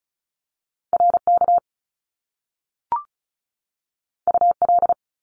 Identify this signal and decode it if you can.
Morse code, heard as a keyed tone, about 36 wpm, 700 Hz. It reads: RK UL
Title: Top Two Letter Combinations in Words 35wpm